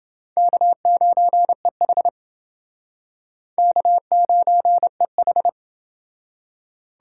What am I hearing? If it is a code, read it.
Morse code, heard as a keyed tone, about 28 wpm, 700 Hz. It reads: K9EH K9EH